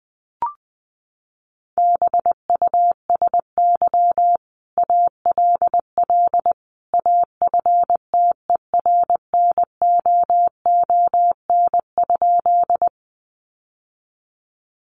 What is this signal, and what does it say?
Morse code, heard as a keyed tone, about 20 wpm, 700 Hz. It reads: BUSY ALL AFTERNOON?
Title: Sentences from Top 1500 Words 20wpm